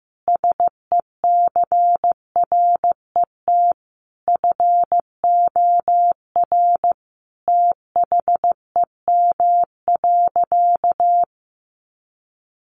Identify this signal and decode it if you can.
Morse code, heard as a keyed tone, about 15 wpm, 700 Hz. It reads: SECRET FOR THEM.
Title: Sentences from Top 1200 Words 15wpm